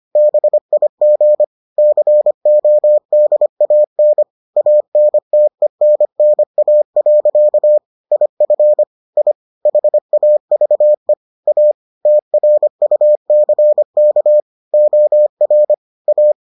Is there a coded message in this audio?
Yes: Morse code from 0.2 to 16.3 s, heard as a keyed tone, about 25 wpm, 600 Hz. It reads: BIG CODAN ANTENNA. IF I HAVE A TRUCK OR A